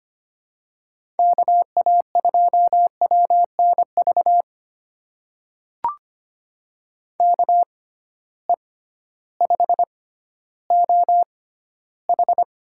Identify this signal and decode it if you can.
Morse code, heard as a keyed tone, about 25 wpm, spaced as for 14 wpm, 700 Hz. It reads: KA2WNV KE5OH
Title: USA Call Signs - 6 Characters - 6x Character Spacing - 25wpm